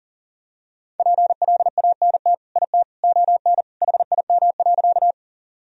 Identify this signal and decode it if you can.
Morse code, heard as a keyed tone, about 40 wpm, 700 Hz. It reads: PLANT IT ON HIM.
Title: Sentences from Top 1000 Words - Review Part 1 - 40wpm